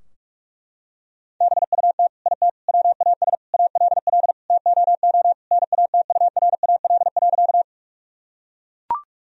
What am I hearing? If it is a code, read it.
Morse code, heard as a keyed tone, about 41 wpm, 700 Hz. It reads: BUT IT WAS ALL TOO NATURAL.
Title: Sentences from Top 1000 Words - Review Part 4 - 45wpm